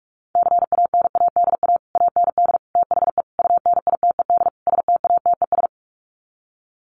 Morse code, heard as a keyed tone, about 44 wpm, 700 Hz. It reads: CANADA AND THE UNITED STATES
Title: Sets of 5 Words Encore 45wpm